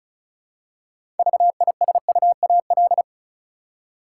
Morse code, heard as a keyed tone, about 34 wpm, 700 Hz. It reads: VISUAL
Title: Six Letter Words - 35wpm